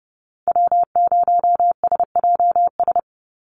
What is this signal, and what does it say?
Morse code, heard as a keyed tone, about 30 wpm, 700 Hz. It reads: W0SJS
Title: Call Signs Repeated 2x - 30wpm